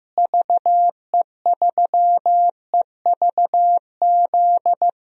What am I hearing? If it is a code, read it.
Morse code, heard as a keyed tone, about 15 wpm, 700 Hz. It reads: VE3EVZ
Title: DXCC-100 Call Signs - 6 Characters - Repeated 3x - 15wpm